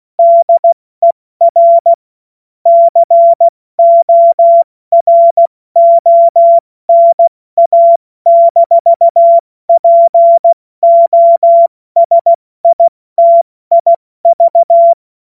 Morse code, heard as a keyed tone, about 16 wpm, spaced as for 14 wpm, 680 Hz. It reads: DER CORONA-POSITIV
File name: Morsekurs / Witze / Witz 22. Juni 2021, 70 F80